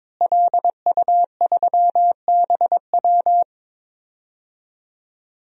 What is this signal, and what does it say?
Morse code, heard as a keyed tone, about 22 wpm, 700 Hz. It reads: LU3BW
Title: DXCC-100 Call Signs - 5 Characters - Repeated 3x - 22wpm